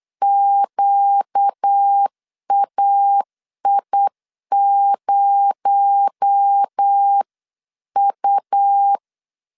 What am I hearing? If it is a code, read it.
Morse code, heard as a keyed tone, about 8 wpm, 800 Hz. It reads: QAI0U
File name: 8wpm